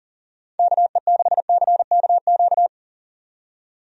Morse code, heard as a keyed tone, about 40 wpm, 700 Hz. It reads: KE6CKQ
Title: USA Call Signs - 6 Characters - 40wpm